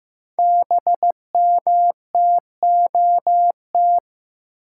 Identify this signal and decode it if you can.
Morse code, heard as a keyed tone, about 15 wpm, 700 Hz. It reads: BMTOT
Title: Code Groups - 5 Letters - 15wpm